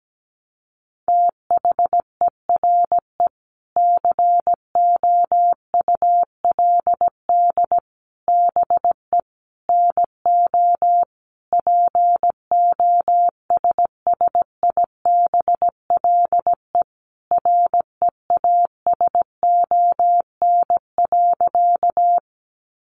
Morse code, heard as a keyed tone, about 17 wpm, 700 Hz. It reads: THERE COULD BE NO POSSIBLE REASON.
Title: Sentences from Top 700 Words 17wpm